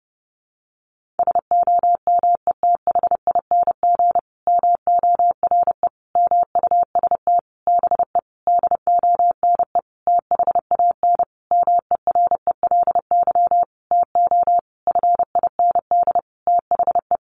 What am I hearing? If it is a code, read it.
Morse code, heard as a keyed tone, about 30 wpm, 700 Hz. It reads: SOMETHING MORE MUST BE DONE THAN MERELY TO FIND THE
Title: Sentences from Top 1400 Words 30wpm